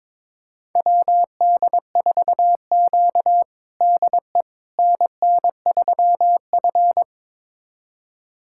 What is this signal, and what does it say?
Morse code, heard as a keyed tone, about 22 wpm, 700 Hz. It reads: WD4Q DE NN3F
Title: USA Callsign DE Callsign - 4 Characters - 22wpm